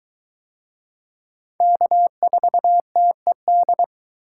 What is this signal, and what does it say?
Morse code, heard as a keyed tone, about 22 wpm, 700 Hz. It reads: K4TED